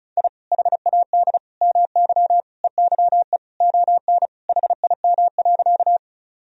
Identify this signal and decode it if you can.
Morse code, heard as a keyed tone, about 34 wpm, 700 Hz. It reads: I HAD MY EYE ON HIM.
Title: Sentences from Top 1000 Words - Review Part 1 - 35wpm